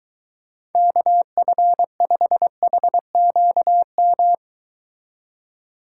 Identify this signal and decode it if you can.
Morse code, heard as a keyed tone, about 23 wpm, 700 Hz. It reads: KF5HQM